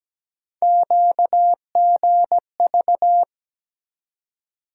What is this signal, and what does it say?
Morse code, heard as a keyed tone, about 17 wpm, 700 Hz. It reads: QGV
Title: USA Call Sign Postfixes - 3 Letters 17wpm